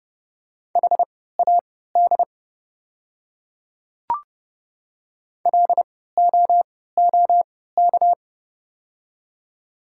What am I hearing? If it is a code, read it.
Morse code, heard as a keyed tone, about 30 wpm, spaced as for 17 wpm, 700 Hz. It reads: HAD LOOK
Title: 3x Character Spacing - Top 200 Words - 30wpm